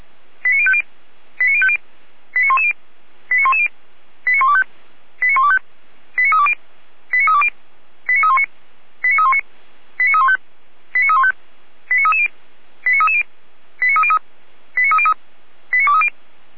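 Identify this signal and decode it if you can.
Five-tone selective call (ZVEI), heard as quick runs of five tones, tones of about 70 ms. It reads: 89050 89050 89110 89110 89125 89125 89320 89320 89319 89319 89315 89315 89330 89330 89303 89303 89230
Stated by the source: Five-tone ZVEI selective call received by SDR, from a German fire brigade.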